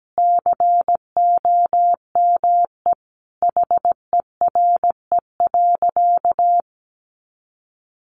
Morse code, heard as a keyed tone, about 17 wpm, 700 Hz. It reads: COME HERE.
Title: Sentences from Top 200 Words 17wpm